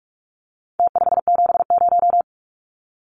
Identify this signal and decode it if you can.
Morse code, heard as a keyed tone, about 46 wpm, 700 Hz. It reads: T570
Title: Numbers - Contest Style - 4-Digit 45wpm